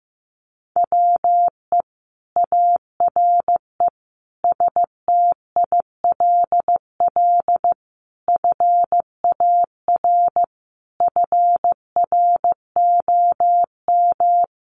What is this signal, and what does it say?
Morse code, heard as a keyed tone, about 14 wpm, 700 Hz. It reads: WE ARE STILL FAR FROM
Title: Sentences from Top 1300 Words 15wpm